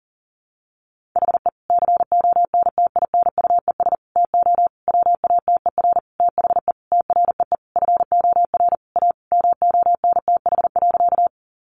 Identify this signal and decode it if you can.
Morse code, heard as a keyed tone, about 40 wpm, 700 Hz. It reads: HE CONTINUES TO WATER THE TREE FOR A MONTH.
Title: Easy Phrases 40wpm